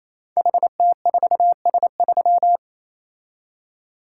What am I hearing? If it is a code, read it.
Morse code, heard as a keyed tone, about 27 wpm, 700 Hz. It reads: HT4S3
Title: Code Groups - 5 Letter:Numbers - 28wpm